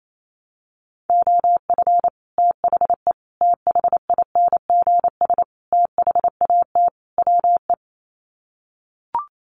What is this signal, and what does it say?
Morse code, heard as a keyed tone, about 28 wpm, 700 Hz. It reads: OF THE THINGS THAT WE
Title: Sets of 5 Words Encore 28wpm